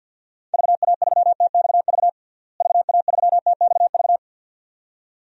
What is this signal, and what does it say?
Morse code, heard as a keyed tone, about 52 wpm, 700 Hz. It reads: VA3TXV VA3TXV
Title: DXCC-100 Call Signs - 6 Characters - Repeated 2x - 50wpm